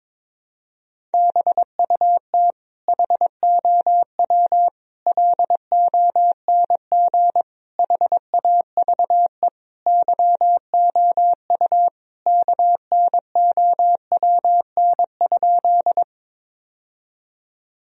Morse code, heard as a keyed tone, about 22 wpm, 700 Hz. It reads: BUT HOW LONG HAVE YOU KNOWN?
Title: Sentences from Top 1000 Words - Review Part 3 - 22wpm